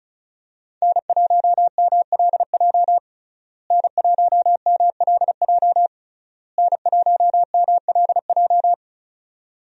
Morse code, heard as a keyed tone, about 36 wpm, 700 Hz. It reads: N1MLJ N1MLJ N1MLJ
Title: USA Call Signs - 5 Characters - Repeated 3x - 35wpm